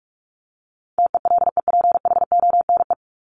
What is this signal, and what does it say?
Morse code, heard as a keyed tone, about 44 wpm, 700 Hz. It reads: TELEPHONE
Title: Top 1501-1750 Words 45wpm